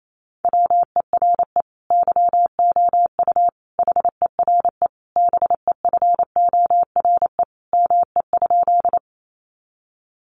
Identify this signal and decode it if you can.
Morse code, heard as a keyed tone, about 28 wpm, 700 Hz. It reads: WERE YOU HERE BEFORE ME?
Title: Sentences from Top 200 Words 28wpm